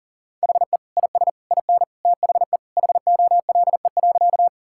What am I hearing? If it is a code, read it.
Morse code, heard as a keyed tone, about 40 wpm, 700 Hz. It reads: HE IS IN THE HOLE.
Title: Sentences from Top 1600 Words 40wpm